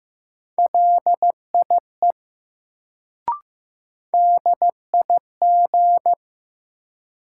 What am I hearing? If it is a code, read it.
Morse code, heard as a keyed tone, about 15 wpm, 700 Hz. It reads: LIE DIG
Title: Sentences from Top 1600 Words - New Word Drill - 15wpm